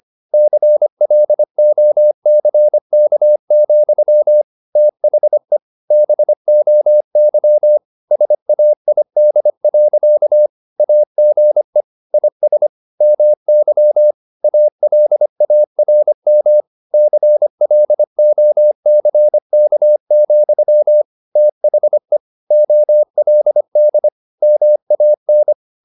Morse code, heard as a keyed tone, about 25 wpm, 600 Hz. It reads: CLOCK, THE BOY SAID. AGE IS MY ALARM CLOCK, THE OLD MAN
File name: the old man and the sea part 1 25wpm